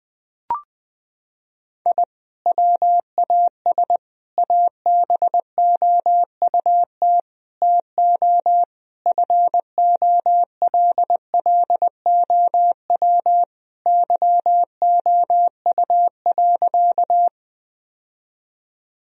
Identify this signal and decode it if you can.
Morse code, heard as a keyed tone, about 20 wpm, 700 Hz. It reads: I WAS ABOUT TO FOLLOW YOU.
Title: Sentences from Top 1000 Words - Review Part 1 - 20wpm